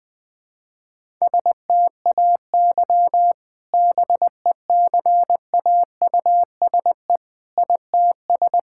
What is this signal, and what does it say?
Morse code, heard as a keyed tone, about 20 wpm, 700 Hz. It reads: STAY BECAUSE ITS
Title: Sentences with 2-Word Phrases - 20wpm